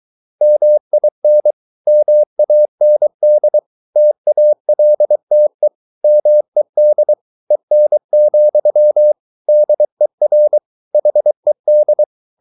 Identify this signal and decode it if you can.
Morse code, heard as a keyed tone, about 23 wpm, 600 Hz. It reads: MIN MAND TALTE MED EN, DER HED
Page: Ferie 23 wpm